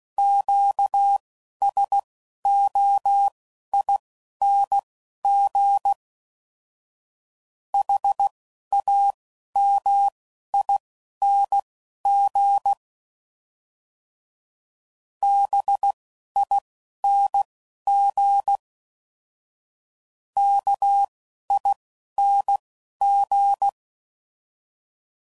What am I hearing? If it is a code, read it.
Morse code, heard as a keyed tone, about 16 wpm, spaced as for 11 wpm, 800 Hz. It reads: QSOING HAMING BING KING
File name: Suffixes / Special INGs / Single INGs